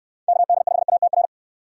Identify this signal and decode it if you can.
Morse code, heard as a keyed tone, about 64 wpm, 700 Hz. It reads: KD5ATR